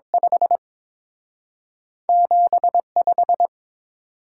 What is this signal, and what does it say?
Morse code, heard as a keyed tone, about 30 wpm, spaced as for 26 wpm, 700 Hz. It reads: 5 75
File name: Speed-Racing - Two Letter-Number Combinations - 22wpm